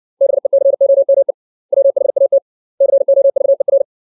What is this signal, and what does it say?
Morse code, heard as a keyed tone, about 60 wpm, 550 Hz. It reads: BECOME WHAT COVER
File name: 632 words / 632 Words 60WPM